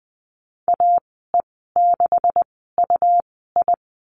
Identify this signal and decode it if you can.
Morse code, heard as a keyed tone, about 20 wpm, spaced as for 15 wpm, 700 Hz. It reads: AE6UI
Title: 140 Lesson 23g - Letter f - Call Signs 20wpm